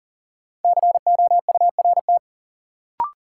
Morse code, heard as a keyed tone, about 40 wpm, 700 Hz. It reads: COURT
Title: Sentences from Top 1600 Words - New Word Drill Alt - 40wpm